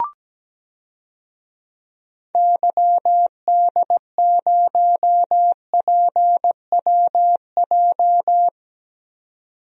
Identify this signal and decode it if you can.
Morse code, heard as a keyed tone, about 17 wpm, 700 Hz. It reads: YD0PWJ
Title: DXCC-100 Call Signs - 6 Characters - 17wpm